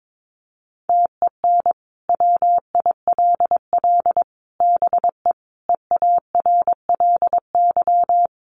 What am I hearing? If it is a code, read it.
Morse code, heard as a keyed tone, about 22 wpm, 700 Hz. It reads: TEN WILL BE EARLY
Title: Sentences from Top 400 Words 22wpm